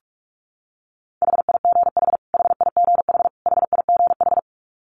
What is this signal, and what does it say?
Morse code, heard as a keyed tone, about 44 wpm, 700 Hz. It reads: HIGH HIGH HIGH
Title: Top 500 Words Repeated 45wpm